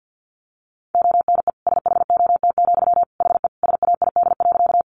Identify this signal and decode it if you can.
Morse code, heard as a keyed tone, about 50 wpm, 700 Hz. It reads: ONE SHOT, HE SAID.
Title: Sentences from Top 1100 Words 50wpm